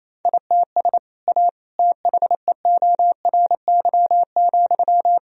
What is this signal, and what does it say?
Morse code, heard as a keyed tone, about 28 wpm, 700 Hz. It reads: ITS A THEORY,
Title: Sentences from Top 1000 Words - Review Part 3 - 28wpm